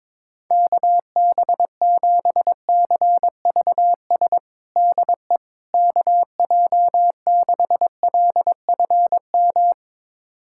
Morse code, heard as a keyed tone, about 22 wpm, 700 Hz. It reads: KB7CVS DE KJ6LFM
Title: USA Callsign DE Callsign - 6 Characters - 22wpm